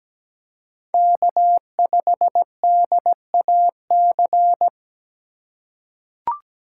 Morse code, heard as a keyed tone, about 17 wpm, 700 Hz. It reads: K5DAC
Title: USA Call Signs - 5 Characters - 8x Character Spacing - 17wpm